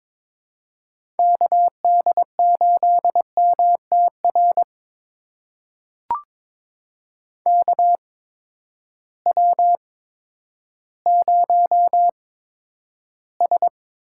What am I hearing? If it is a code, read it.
Morse code, heard as a keyed tone, about 22 wpm, spaced as for 11 wpm, 700 Hz. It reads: KD8MTR KW0S
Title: USA Call Signs - 8x Character Spacing - 22wpm